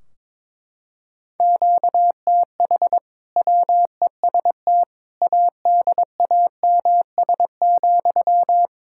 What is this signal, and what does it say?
Morse code, heard as a keyed tone, about 22 wpm, 700 Hz. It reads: QTH WEST ADAMS,
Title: Practice QSO - Exchange 1 - Beginning - 22wpm